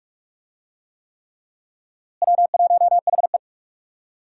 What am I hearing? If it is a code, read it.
Morse code, heard as a keyed tone, about 41 wpm, 700 Hz. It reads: W1HE
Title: USA Call Signs - 4 Characters - Repeated 3x - 45wpm